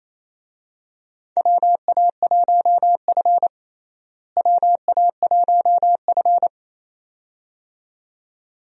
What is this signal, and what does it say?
Morse code, heard as a keyed tone, about 28 wpm, 700 Hz. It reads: WA1F WA1F
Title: USA Call Signs - 4 Characters - Repeated 2x - 28wpm